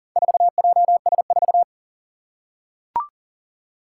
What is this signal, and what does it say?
Morse code, heard as a keyed tone, about 40 wpm, 690 Hz. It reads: 4JS4
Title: Four Challenging Letter-Number Combinations 40wpm